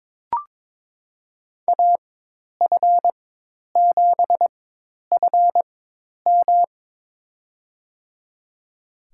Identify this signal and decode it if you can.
Morse code, heard as a keyed tone, about 22 wpm, spaced as for 10 wpm, 700 Hz. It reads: AF7FM
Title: USA Call Signs - 5 Characters - 4x Character Spacing - 22wpm